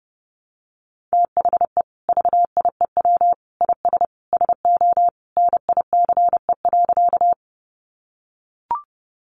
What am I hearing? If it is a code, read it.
Morse code, heard as a keyed tone, about 30 wpm, 700 Hz. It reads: THE VIEW IS SO NICE.
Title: Additional Easy Phrases 30wpm